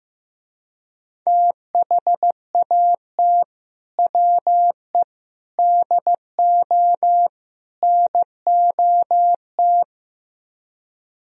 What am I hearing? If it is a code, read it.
Morse code, heard as a keyed tone, about 15 wpm, 700 Hz. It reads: THAT WE DO NOT
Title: Speed-Racing - Sets of 4 Words 15wpm